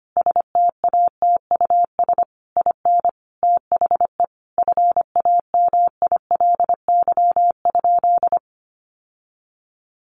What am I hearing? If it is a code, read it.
Morse code, heard as a keyed tone, about 25 wpm, 700 Hz. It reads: STATUS IN THE FAMILY?
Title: Sentences from Top 1400 Words 25wpm